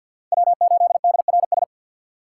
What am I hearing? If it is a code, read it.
Morse code, heard as a keyed tone, about 48 wpm, 700 Hz. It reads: W8DRS
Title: USA Call Signs - 4x Character Spacing - 50wpm